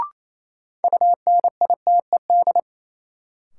Morse code, heard as a keyed tone, about 28 wpm, 700 Hz. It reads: UNITED
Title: Sentences from Top 600 Words - New Word Drill - 28wpm